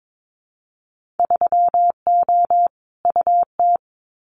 Morse code, heard as a keyed tone, about 22 wpm, 700 Hz. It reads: 3O UT
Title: Winter Field Day Exchanges 22wpm